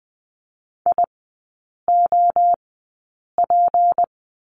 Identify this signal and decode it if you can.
Morse code, heard as a keyed tone, about 20 wpm, spaced as for 8 wpm, 700 Hz. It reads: IOP
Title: 152 Lesson 25e - Letter p - Three Characters 20wpm